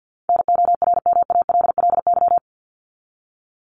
Morse code, heard as a keyed tone, about 50 wpm, 700 Hz. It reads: NORMALLY
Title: Top 751-1000 Words 50wpm